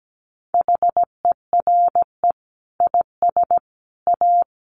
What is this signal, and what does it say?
Morse code, heard as a keyed tone, about 17 wpm, 700 Hz. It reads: HERE IS A
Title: Sentences from Top 1100 Words 17wpm